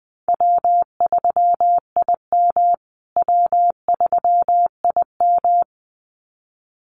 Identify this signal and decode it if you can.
Morse code, heard as a keyed tone, about 20 wpm, 700 Hz. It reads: W3IM W3IM
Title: USA Call Signs Repeated 2x 20wpm